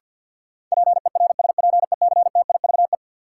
Morse code, heard as a keyed tone, about 52 wpm, 700 Hz. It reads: PERSPECTIVE
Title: Sentences from Top 1200 Words - New Word Drill Alt - 50wpm